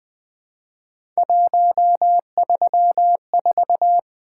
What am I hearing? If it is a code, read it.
Morse code, heard as a keyed tone, about 20 wpm, 700 Hz. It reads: 134
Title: Three Letter-Number Combinations - 20wpm